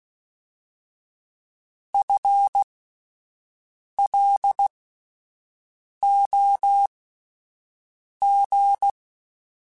Morse code, heard as a keyed tone, about 16 wpm, spaced as for 5 wpm, 800 Hz. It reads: FLOG
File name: Spaced / FL Runs sp mp3s / FL Runs sp @10